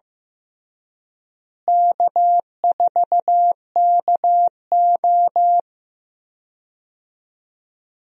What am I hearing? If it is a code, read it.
Morse code, heard as a keyed tone, about 15 wpm, 700 Hz. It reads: K4KO
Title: USA Call Signs - 4 Characters - Repeated 2x - 15wpm